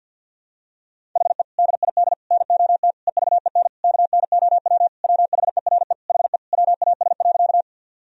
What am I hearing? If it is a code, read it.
Morse code, heard as a keyed tone, about 52 wpm, 700 Hz. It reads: HE DID NOT EVEN KNOW WHERE HE WAS.
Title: Sentences from Top 1000 Words - Review Part 1 - 50wpm